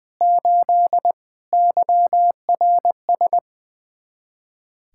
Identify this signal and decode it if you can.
Morse code, heard as a keyed tone, about 25 wpm, 700 Hz. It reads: 8 YRS
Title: Speed-Racing Common QSO Phrases - 20wpm